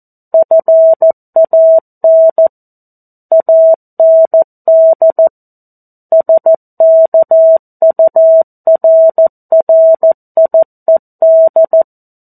Morse code, heard as a keyed tone, about 14 wpm, 650 Hz. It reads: FAN AND SKURRIED
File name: 02-Chapter-02 14 WPM